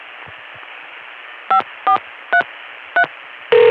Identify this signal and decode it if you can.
DTMF, heard as telephone touch tones, tones of about 90 ms. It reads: 5433